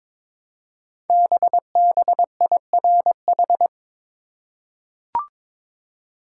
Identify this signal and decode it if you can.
Morse code, heard as a keyed tone, about 22 wpm, 700 Hz. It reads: BBIRH
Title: Code Groups - 5 Letters - 22wpm